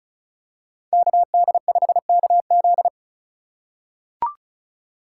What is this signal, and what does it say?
Morse code, heard as a keyed tone, about 34 wpm, 700 Hz. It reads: KD5KZ
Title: USA Call Signs - 5 Characters - 4x Character Spacing - 35wpm